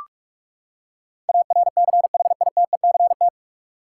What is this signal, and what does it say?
Morse code, heard as a keyed tone, about 46 wpm, 700 Hz. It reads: ARCHITECT
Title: Practice QSO - Exchange 3 - More Info - 45wpm